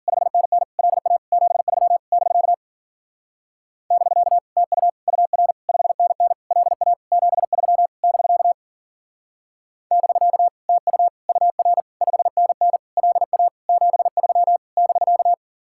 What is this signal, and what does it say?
Morse code, heard as a keyed tone, about 45 wpm, 700 Hz. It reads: 5NN LA 73 <BK> <BK> TU UR 5NN LA 73 <BK> <BK> TU UR 5NN LA 73 <BK>
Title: Speed-Racing Common QSO Phrases Encore - 40wpm